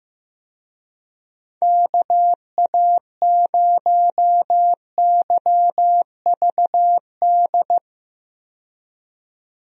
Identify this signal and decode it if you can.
Morse code, heard as a keyed tone, about 15 wpm, 700 Hz. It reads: KA0YVD
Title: USA Call Signs - 6 Characters - 15wpm